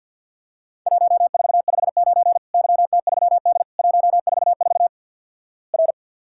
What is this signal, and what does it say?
Morse code, heard as a keyed tone, about 47 wpm, 700 Hz. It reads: 1459 YT3D 144 R
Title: Worked All Europe DX Contest - QTC Exchange - 50wpm